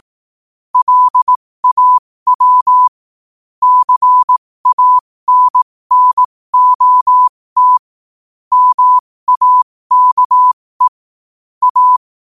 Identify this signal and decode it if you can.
Morse code, heard as a keyed tone, about 18 wpm, spaced as for 15 wpm, 1000 Hz. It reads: LAW CANNOT MAKE A
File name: qotdfiles / 15wpm